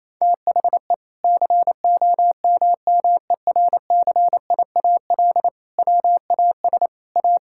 Morse code, heard as a keyed tone, about 28 wpm, 700 Hz. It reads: THE COMMERCIAL WAS A
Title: Sentences from Top 1000 Words - Review Part 5 - 28wpm